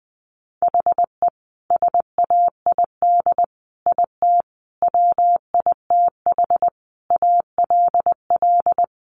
Morse code, heard as a keyed tone, about 20 wpm, 700 Hz. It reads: HE SAID IT WITH ALL
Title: Sentences from Top 1200 Words 20wpm